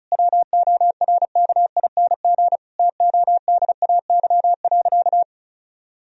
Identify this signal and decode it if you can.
Morse code, heard as a keyed tone, about 36 wpm, 700 Hz. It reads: WORKING TODAY.
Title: Sentences from Top 1000 Words - Review Part 4 - 35wpm